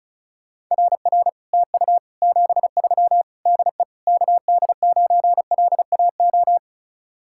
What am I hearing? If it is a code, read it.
Morse code, heard as a keyed tone, about 45 wpm, 700 Hz. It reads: RR TU 73 DE KD9LAO